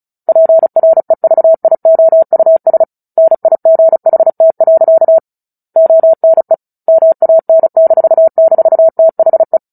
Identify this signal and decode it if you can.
Morse code, heard as a keyed tone, about 34 wpm, 650 Hz. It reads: PREVIOUS NIGHT. ONE MAN--THE
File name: Book Two / 09-Chapter 9 - 35 WPM